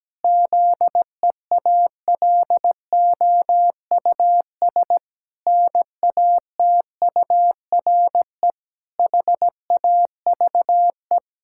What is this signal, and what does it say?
Morse code, heard as a keyed tone, about 17 wpm, 700 Hz. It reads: ZEALOUS NATURE HAVE
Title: Pangram Challenge v2 - 17wpm